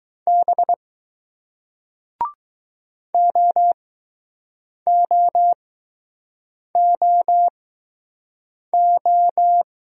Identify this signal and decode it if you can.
Morse code, heard as a keyed tone, about 20 wpm, spaced as for 6 wpm, 700 Hz. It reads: B OOOO